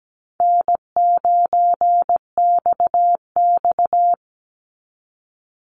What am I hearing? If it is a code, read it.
Morse code, heard as a keyed tone, about 17 wpm, 700 Hz. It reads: N9XX
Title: Call Signs Repeated 2x - 17wpm